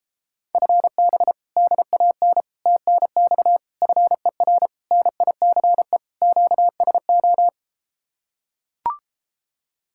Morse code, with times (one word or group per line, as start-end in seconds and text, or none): 0.5-1.3 FB
1.6-2.4 DAN
2.7-3.6 TNX
3.8-4.7 FER
4.9-6.0 NICE
6.2-7.5 QSO